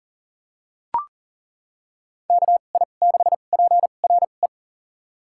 Morse code, heard as a keyed tone, about 40 wpm, spaced as for 29 wpm, 700 Hz. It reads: KI6PRE
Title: USA Call Signs - 6 Characters - 2x Character Spacing - 40wpm